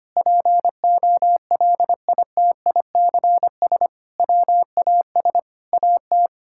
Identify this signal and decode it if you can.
Morse code, heard as a keyed tone, about 25 wpm, 700 Hz. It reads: POLITICS WAS AT
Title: Sentences from Top 1000 Words - Review Part 6 - 25wpm